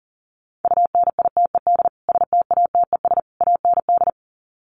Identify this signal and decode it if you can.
Morse code, heard as a keyed tone, about 40 wpm, 700 Hz. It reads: UNITED STATES AND
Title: Sets of 3 Words 40wpm